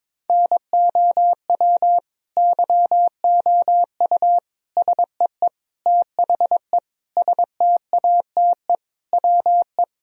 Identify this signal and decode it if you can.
Morse code, heard as a keyed tone, about 22 wpm, 700 Hz. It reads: NOW YOU SEE THE STATE WE
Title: Sentences from Top 1000 Words - Review Part 2 - 22wpm